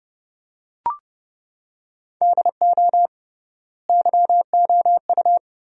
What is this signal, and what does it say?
Morse code, heard as a keyed tone, about 30 wpm, 700 Hz. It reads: DO YOU
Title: 3x Word-Spacing - Sentences from Top 400 Words - 30wpm